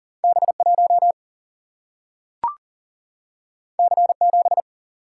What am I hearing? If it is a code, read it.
Morse code, heard as a keyed tone, about 40 wpm, 700 Hz. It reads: B1 C7
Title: Single-Letter Single-Number 40wpm